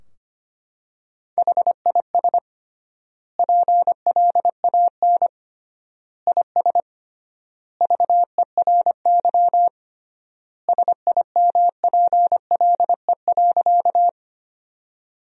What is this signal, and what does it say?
Morse code, heard as a keyed tone, about 25 wpm, 700 Hz. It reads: HIS PLAN IS VERY SIMPLE.
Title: 3x Word-Spacing - Sentences from Top 500 Words - 25wpm